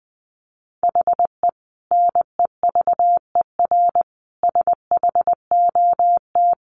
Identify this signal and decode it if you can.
Morse code, heard as a keyed tone, about 20 wpm, 700 Hz. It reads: HE NEVER SHOT
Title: Sentences from Top 1100 Words 20wpm